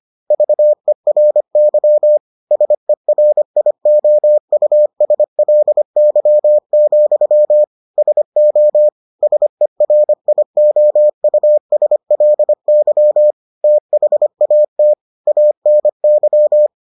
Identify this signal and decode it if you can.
Morse code, heard as a keyed tone, about 25 wpm, 600 Hz. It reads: VERY SERIOUSLY, SO SERIOUSLY THAT ANY